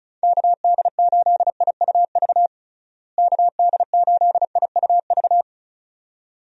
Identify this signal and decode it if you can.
Morse code, heard as a keyed tone, about 36 wpm, 700 Hz. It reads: KD8IUV KD8IUV
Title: USA Call Signs - 6 Characters - Repeated 2x - 35wpm